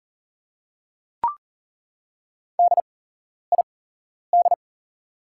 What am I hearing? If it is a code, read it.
Morse code, heard as a keyed tone, about 40 wpm, spaced as for 11 wpm, 700 Hz. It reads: DID
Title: 8x Character Spacing - Top 200 Words - 40wpm